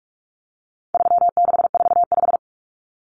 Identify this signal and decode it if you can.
Morse code, heard as a keyed tone, about 44 wpm, 700 Hz. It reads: 3645
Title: Common Cut Numbers - 4-Digit 45wpm